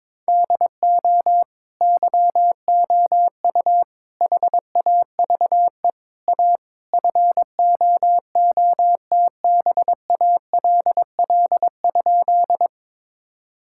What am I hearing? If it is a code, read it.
Morse code, heard as a keyed tone, about 22 wpm, 700 Hz. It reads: DO YOU HAVE A FOOTBALL?
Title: Sentences from Top 1600 Words 22wpm